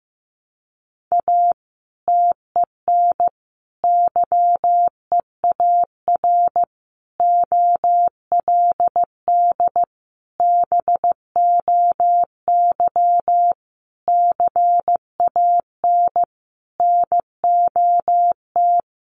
Morse code, heard as a keyed tone, about 15 wpm, 700 Hz. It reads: A TEN YEAR OLD BOY CAN NOT
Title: Sentences from Top 1300 Words 15wpm